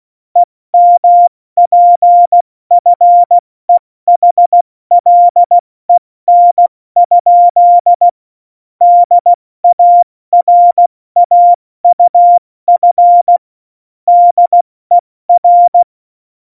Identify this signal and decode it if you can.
Morse code, heard as a keyed tone, about 16 wpm, 700 Hz. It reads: EMPFEHLEN? DARAUF DER
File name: Morsekurs / Witze / Witz 20. Oktober 2020, 70 F80